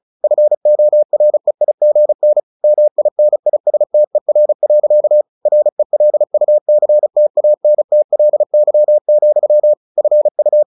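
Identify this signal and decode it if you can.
Morse code, heard as a keyed tone, about 34 wpm, 600 Hz. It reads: FOREIGN MINISTER. RELUCTANTLY, FU
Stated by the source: Morse practice at 35 wpm from CW med Gnister, sending Ken Follett's novel Never in English.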